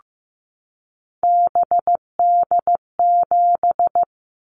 Morse code, heard as a keyed tone, about 15 wpm, 700 Hz. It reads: BD7
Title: Callsign Prefixes for Top 100 DXCC - 15wpm